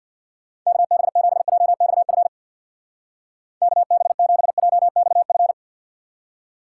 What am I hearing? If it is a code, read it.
Morse code, heard as a keyed tone, about 63 wpm, 700 Hz. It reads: KB7JXF KB7JXF